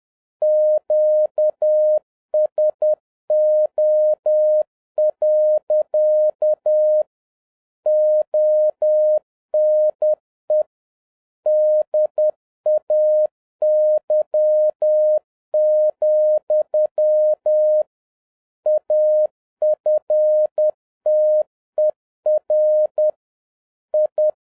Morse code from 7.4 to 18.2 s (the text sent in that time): ONE DAY,